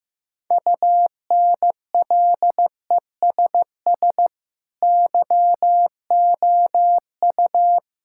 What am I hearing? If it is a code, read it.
Morse code, heard as a keyed tone, about 15 wpm, 700 Hz. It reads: UNLESS YOU
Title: Sentences from Top 1000 Words - Review Part 6 - 15wpm